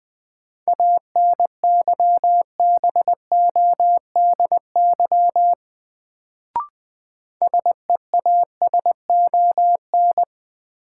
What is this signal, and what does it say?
Morse code, heard as a keyed tone, about 20 wpm, 700 Hz. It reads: ANYBODY SEASON
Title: Sets of 5 Words - New Word Drill - 20wpm